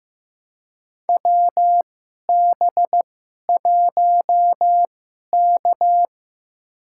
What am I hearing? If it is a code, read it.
Morse code, heard as a keyed tone, about 15 wpm, spaced as for 11 wpm, 700 Hz. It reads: WB1K
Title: USA Call Signs - 2x Character Spacing - 15wpm